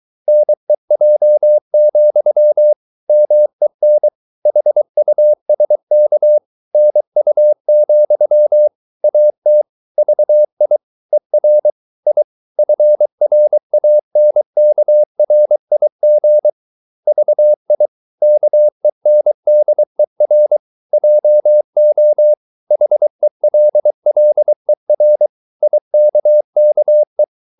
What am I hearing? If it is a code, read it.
Morse code, heard as a keyed tone, about 23 wpm, 600 Hz. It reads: NEJ, MEN HUSK NU, AT VI ER I FRANKRIG VI KENDER JO HELLER IKKE
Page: Ferie 23 wpm